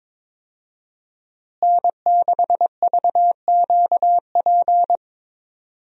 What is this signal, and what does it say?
Morse code, heard as a keyed tone, about 21 wpm, 700 Hz. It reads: N6VQP
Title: USA Call Signs - 5 Characters - Repeated 3x - 22wpm